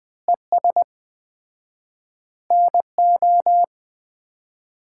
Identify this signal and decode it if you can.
Morse code, heard as a keyed tone, about 20 wpm, 700 Hz. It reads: ES NO
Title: Common QSO Phrases - 4x Word-Spacing - 20wpm